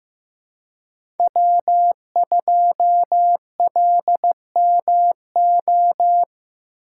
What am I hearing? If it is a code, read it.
Morse code, heard as a keyed tone, about 15 wpm, 700 Hz. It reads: W2LMO
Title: USA Call Signs - 5 Characters - 4x Character Spacing - 15wpm